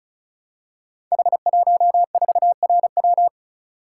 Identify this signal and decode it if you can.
Morse code, heard as a keyed tone, about 36 wpm, 690 Hz. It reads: H14RW
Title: Code Groups - 5 Letter:Numbers - 35wpm